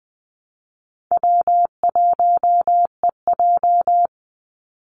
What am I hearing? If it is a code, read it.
Morse code, heard as a keyed tone, about 20 wpm, 700 Hz. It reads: W1EJ
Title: NA SOTA Activator Callsigns 20wpm